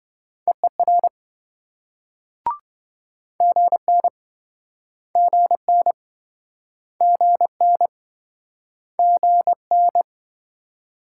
Morse code, heard as a keyed tone, about 28 wpm, 700 Hz. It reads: EER GN GN GN GN